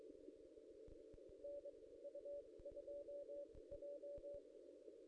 Morse code, heard as a keyed tone, about 23 wpm, spaced as for 18 wpm, 570 Hz, with noise. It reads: NU2J
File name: contests / ss-cw-k5tr-op-n5rz -1700